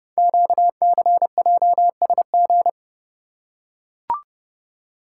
Morse code, heard as a keyed tone, about 29 wpm, 700 Hz. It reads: QCJSG
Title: Code Groups - 5 Letters - 30wpm